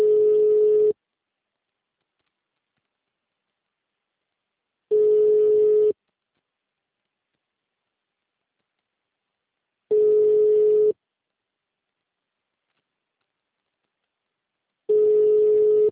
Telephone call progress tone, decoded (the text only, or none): ringback tone